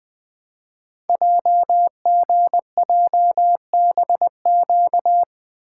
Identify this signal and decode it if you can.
Morse code, heard as a keyed tone, about 20 wpm, 700 Hz. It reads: JGJBQ
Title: Code Groups - 5 Letters - 20wpm